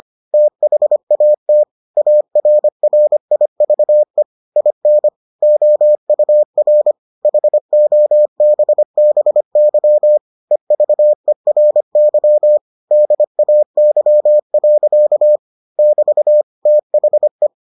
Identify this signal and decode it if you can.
Morse code, heard as a keyed tone, about 25 wpm, 600 Hz. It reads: THAT ARRIVE IN OUR HOBBY EVERY DAY. <BT> THE